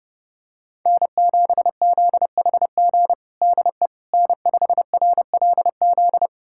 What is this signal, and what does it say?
Morse code, heard as a keyed tone, about 30 wpm, 700 Hz. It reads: N7ZHG DE N5RLZ
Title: USA Callsign DE Callsign - 5 Characters - 30wpm